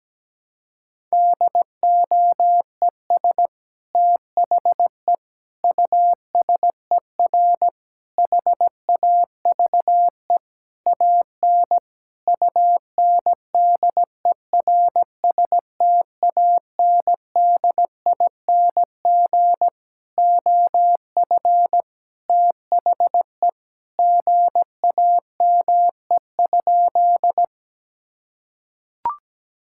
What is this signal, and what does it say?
Morse code, heard as a keyed tone, about 17 wpm, 700 Hz. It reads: DOES THE USER HAVE AN UNDERSTANDING OF THE GAME?
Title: Sentences from Top 1000 Words - Review Part 5 - 17wpm